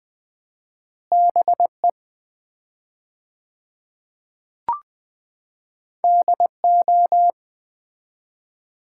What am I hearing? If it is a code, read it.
Morse code, heard as a keyed tone, about 21 wpm, 700 Hz. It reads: BE DO
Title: 4x Word-Spacing - Sets of 5 Words - 20wpm